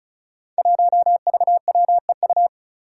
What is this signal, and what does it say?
Morse code, heard as a keyed tone, about 36 wpm, 700 Hz. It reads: 1VWEU
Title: Five Challenging Letter-Number Combinations 35wpm